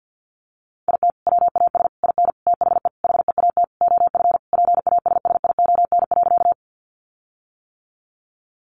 Morse code, heard as a keyed tone, about 50 wpm, 700 Hz. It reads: IT WAS IN THE HEAT OF PASSION.
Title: Sentences from Top 1300 Words 50wpm